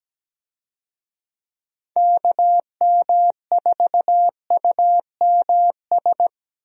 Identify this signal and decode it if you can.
Morse code, heard as a keyed tone, about 17 wpm, 700 Hz. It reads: KM4UMS
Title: USA Call Signs - 6 Characters - Repeated 2x - 17wpm